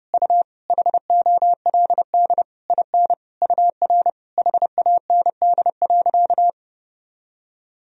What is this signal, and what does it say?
Morse code, heard as a keyed tone, about 30 wpm, 700 Hz. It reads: U HOLD IN UR HAND.
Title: Sentences from Top 400 Words and Abbreviations 30wpm